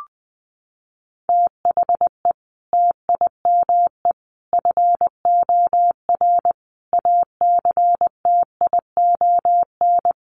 Morse code, heard as a keyed tone, about 20 wpm, 700 Hz. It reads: THE TIME FOR ACTION
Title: Sentences from Top 700 Words 20wpm